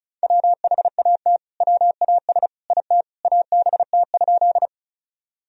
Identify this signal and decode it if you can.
Morse code, heard as a keyed tone, about 36 wpm, 700 Hz. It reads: WHAT WAS IT ABT?
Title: Sentences from Top 100 Words and Abbreviations 35wpm